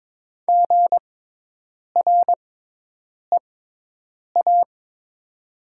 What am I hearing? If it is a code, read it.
Morse code, heard as a keyed tone, about 22 wpm, spaced as for 8 wpm, 700 Hz. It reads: GREA
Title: 6x Character Spacing - Top 300 Words - 22wpm